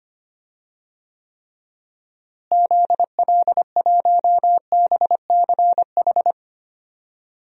Morse code, heard as a keyed tone, about 25 wpm, 700 Hz. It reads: ZL1BCH
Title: DXCC-100 Call Signs - 6 Characters - Repeated 2x - 25wpm